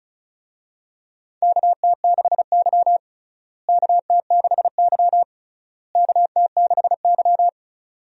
Morse code, heard as a keyed tone, about 36 wpm, 700 Hz. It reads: KT6Y KT6Y KT6Y
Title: USA Call Signs - 4 Characters - Repeated 3x - 35wpm